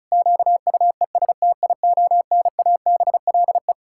Morse code, heard as a keyed tone, about 34 wpm, 700 Hz. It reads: QUESTIONABLE
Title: Pangram Challenge v2 - 35wpm